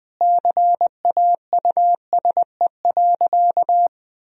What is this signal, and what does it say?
Morse code, heard as a keyed tone, about 21 wpm, 700 Hz. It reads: CAUSE.
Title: Sentences from Top 1000 Words - Review Part 1 - 20wpm